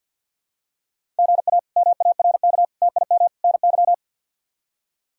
Morse code, heard as a keyed tone, about 53 wpm, 700 Hz. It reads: GA MARK TIM NY
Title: K1USN SST QSO Practice - 50wpm